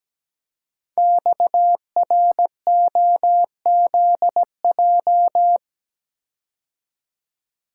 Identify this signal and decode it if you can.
Morse code, heard as a keyed tone, about 17 wpm, 700 Hz. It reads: XROZJ
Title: Code Groups - 5 Letters - 17wpm